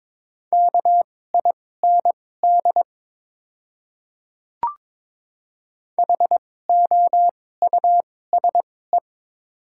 Morse code, heard as a keyed tone, about 22 wpm, spaced as for 16 wpm, 700 Hz. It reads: KIND HOUSE
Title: 2x Character Spacing - Top 200 Words - 22wpm